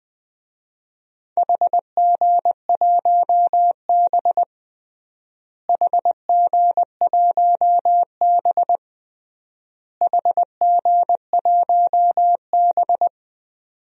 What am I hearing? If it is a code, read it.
Morse code, heard as a keyed tone, about 20 wpm, 700 Hz. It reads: HG1B HG1B HG1B
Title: DXCC-100 Call Signs - 4 Characters - Repeated 3x - 20wpm